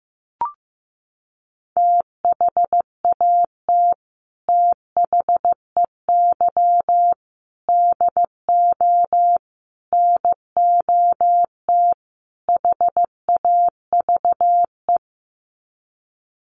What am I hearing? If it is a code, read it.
Morse code, heard as a keyed tone, about 15 wpm, 700 Hz. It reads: THAT THEY DO NOT HAVE
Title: Sets of 5 Words Encore2 15wpm